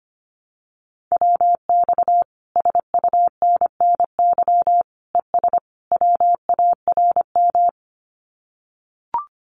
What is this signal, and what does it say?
Morse code, heard as a keyed tone, about 25 wpm, 700 Hz. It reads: WX SUNNY ES WARM
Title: Common QSO Phrases Encore 25wpm